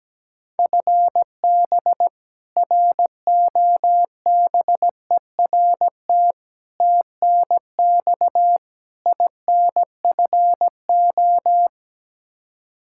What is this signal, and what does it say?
Morse code, heard as a keyed tone, about 17 wpm, 700 Hz. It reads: FB ROBERT TNX INFO
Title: K1USN SST QSO Practice 17wpm